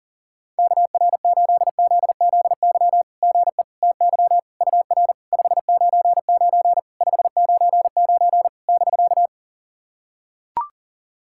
Morse code, heard as a keyed tone, about 40 wpm, 690 Hz. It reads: KR8ZZY GE TY UR 599 599 <BK>
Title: SOTA QSO Practice 40wpm